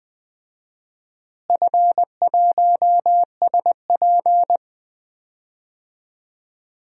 Morse code, heard as a keyed tone, about 21 wpm, 700 Hz. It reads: F1SP
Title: DXCC-100 Call Signs - 4 Characters - Repeated 2x - 20wpm